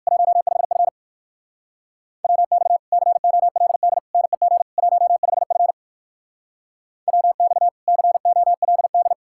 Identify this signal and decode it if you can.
Morse code, heard as a keyed tone, about 55 wpm, 700 Hz. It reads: 15F WX COLD NEG 15F WX COLD
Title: Speed-Racing Common QSO Phrases - 45wpm